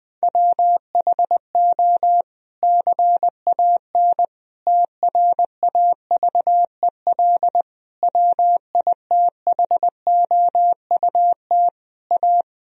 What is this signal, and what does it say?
Morse code, heard as a keyed tone, about 21 wpm, 700 Hz. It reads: WHO CAN TRAVEL WITHOUT A
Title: Sentences from Top 1000 Words - Review Part 3 - 20wpm